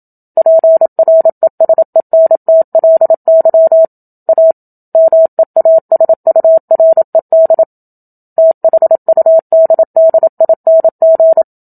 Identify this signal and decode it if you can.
Morse code, heard as a keyed tone, about 27 wpm, 650 Hz. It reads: PRESENTLY A MEASURED THUDDING
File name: Book Two / 01-Chapter 1 - 27 WPM